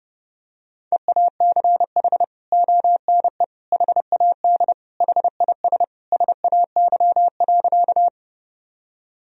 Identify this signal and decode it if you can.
Morse code, heard as a keyed tone, about 30 wpm, 700 Hz. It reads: EACH ONE HAD HIS SAY.
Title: Sentences from Top 1000 Words - Review Part 1 - 30wpm